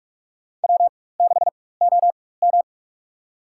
Morse code, heard as a keyed tone, about 46 wpm, spaced as for 21 wpm, 700 Hz. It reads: W6OM
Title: USA Call Signs - 4 Characters - 4x Character Spacing - 45wpm